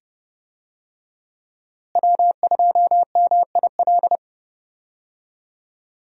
Morse code, heard as a keyed tone, about 30 wpm, 700 Hz. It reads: W2MIL
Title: USA Call Signs - 5 Characters - Repeated 2x - 30wpm